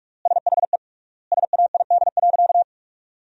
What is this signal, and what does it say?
Morse code, heard as a keyed tone, about 46 wpm, 700 Hz. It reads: SHE SAID.
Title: 3x Word-Spacing - Sentences from Top 400 Words - 45wpm